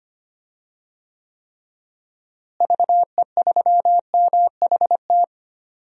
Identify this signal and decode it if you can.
Morse code, heard as a keyed tone, about 25 wpm, 700 Hz. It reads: VE3MHT
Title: DXCC-100 Call Signs - 6 Characters - Repeated 3x - 25wpm